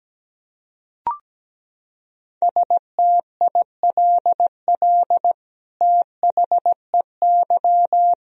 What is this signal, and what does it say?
Morse code, heard as a keyed tone, about 17 wpm, 700 Hz. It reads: STILL THEY
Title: Sentences from Top 1000 Words - Review Part 1 - 17wpm